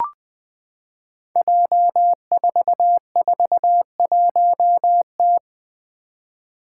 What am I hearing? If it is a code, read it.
Morse code, heard as a keyed tone, about 20 wpm, 700 Hz. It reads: J441T
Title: Five Challenging Letter-Number Combinations 20wpm